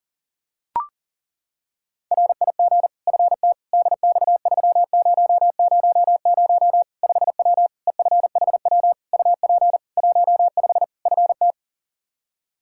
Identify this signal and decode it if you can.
Morse code, heard as a keyed tone, about 37 wpm, 700 Hz. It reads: RIG FT DX3000 5W EFHW UP 15 FT
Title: K1USN SST QSO Practice 40wpm